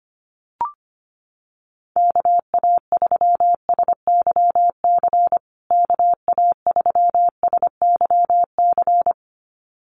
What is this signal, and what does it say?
Morse code, heard as a keyed tone, about 25 wpm, 700 Hz. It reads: KA3SYC KA3SYC
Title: USA Call Signs Repeated 2x 25wpm